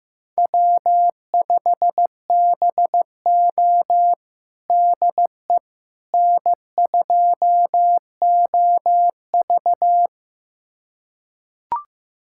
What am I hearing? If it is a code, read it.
Morse code, heard as a keyed tone, about 16 wpm, 700 Hz. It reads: W5BO DE N2OV
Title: USA Callsign DE Callsign - 4 Characters - 15wpm